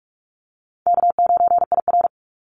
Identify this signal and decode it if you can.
Morse code, heard as a keyed tone, about 46 wpm, 700 Hz. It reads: K9IR
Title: NA SOTA Activator Callsigns 45wpm